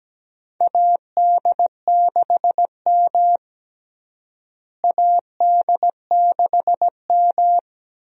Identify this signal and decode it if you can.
Morse code, heard as a keyed tone, about 17 wpm, 700 Hz. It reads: AD6M AD6M
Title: USA Call Signs - 4 Characters - Repeated 2x - 17wpm